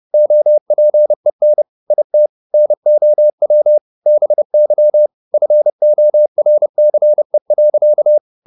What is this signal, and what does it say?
Morse code, heard as a keyed tone, about 30 wpm, 600 Hz. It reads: OPEN IT NOW BY FORCE.
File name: the old man and the sea / the old man and the sea 1 30wpm